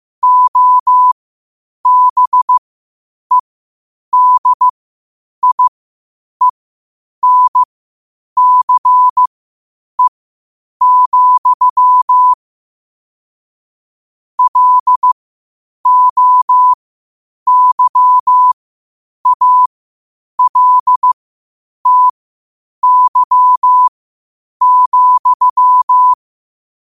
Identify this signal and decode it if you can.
Morse code, heard as a keyed tone, about 15 wpm, spaced as for 8 wpm, 1000 Hz. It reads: OBEDIENCE, LOYALTY,
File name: qotdfiles / 8wpm